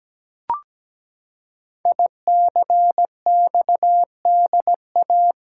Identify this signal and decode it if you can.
Morse code, heard as a keyed tone, about 17 wpm, 700 Hz. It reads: ICXDA
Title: Code Groups - 5 Letter:Numbers - 17wpm